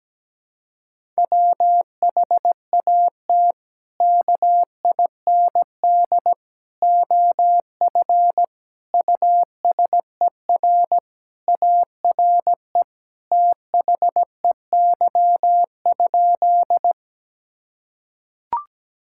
Morse code, heard as a keyed tone, about 17 wpm, 700 Hz. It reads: WHAT KIND OF USER ARE THEY?
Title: Sentences from Top 1000 Words - Review Part 5 - 17wpm